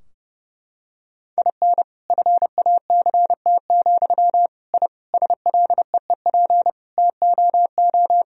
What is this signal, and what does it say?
Morse code, heard as a keyed tone, about 30 wpm, 700 Hz. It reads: IN FACT, I SLEEP TOO
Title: Sentences from Top 1000 Words - Review Part 3 - 30wpm